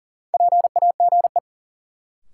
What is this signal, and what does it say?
Morse code, heard as a keyed tone, about 40 wpm, 700 Hz. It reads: PAGE
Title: Sets of 5 Words Encore - New Word Drill - 40wpm